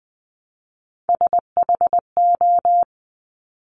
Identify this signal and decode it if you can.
Morse code, heard as a keyed tone, about 20 wpm, 700 Hz. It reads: SHO
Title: Top Three Letter Combinations in Words 20wpm